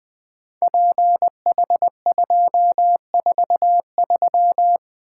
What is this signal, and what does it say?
Morse code, heard as a keyed tone, about 20 wpm, 700 Hz. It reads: PH243
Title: Code Groups - 5 Letter:Numbers - 20wpm